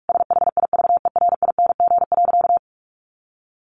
Morse code, heard as a keyed tone, about 45 wpm, 700 Hz. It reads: SHIVERING.
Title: A Sound Of Thunder 45wpm